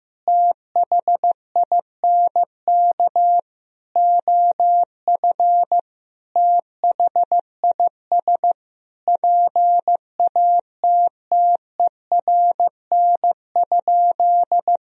Morse code, heard as a keyed tone, about 15 wpm, 700 Hz. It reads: THINK OF THIS PATTERN?
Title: Sentences from Top 1000 Words - Review Part 2 - 15wpm